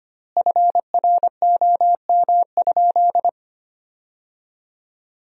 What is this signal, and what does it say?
Morse code, heard as a keyed tone, about 25 wpm, 700 Hz. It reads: FROM?
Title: 3x Word-Spacing - Sentences from Top 200 Words - 25wpm